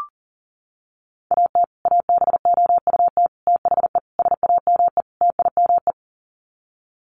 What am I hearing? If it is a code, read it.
Morse code, heard as a keyed tone, about 40 wpm, 700 Hz. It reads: AT ABOUT THE SAME TIME
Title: Sets of 5 Words Encore 40wpm